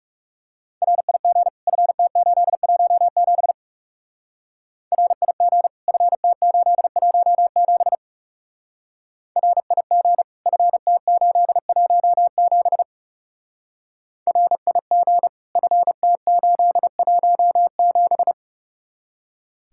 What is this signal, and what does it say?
Morse code, heard as a keyed tone, about 36 wpm, 700 Hz. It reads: RIG FT817 RIG FT817 RIG FT817 RIG FT817
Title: Speed-Racing Common QSO Phrases Encore - 30wpm